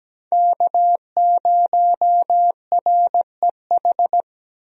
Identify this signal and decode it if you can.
Morse code, heard as a keyed tone, about 17 wpm, 700 Hz. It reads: K0REH
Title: USA Call Signs - 5 Characters - Repeated 2x - 17wpm